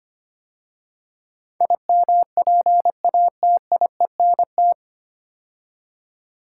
Morse code, heard as a keyed tone, about 25 wpm, 700 Hz. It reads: IMPATIENT
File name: practice / 25wpm / 9 Letter Words - First 3 Letters Given - 25wpm